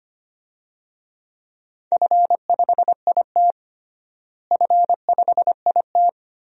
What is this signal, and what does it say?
Morse code, heard as a keyed tone, about 25 wpm, 700 Hz. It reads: F5IT F5IT
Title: DXCC-100 Call Signs - 4 Characters - Repeated 2x - 25wpm